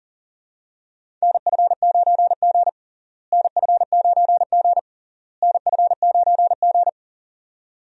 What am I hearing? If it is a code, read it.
Morse code, heard as a keyed tone, about 40 wpm, 700 Hz. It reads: NF9G NF9G NF9G
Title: USA Call Signs - 4 Characters - Repeated 3x - 40wpm